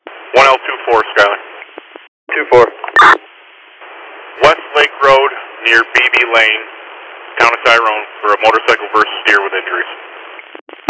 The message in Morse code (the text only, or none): none